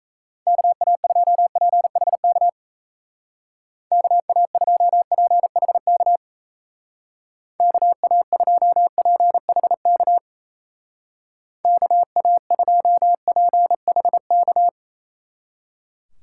Morse code, heard as a keyed tone, about 36 wpm, 700 Hz. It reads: KA2PHK KA2PHK KA2PHK KA2PHK